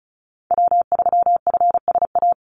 Morse code, heard as a keyed tone, about 34 wpm, 700 Hz. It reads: W3FSA
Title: Call Signs - 35wpm